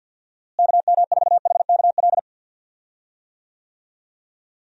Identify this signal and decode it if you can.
Morse code, heard as a keyed tone, about 46 wpm, 700 Hz. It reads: KM4HKL
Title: USA Call Signs - 6 Characters - Repeated 2x - 50wpm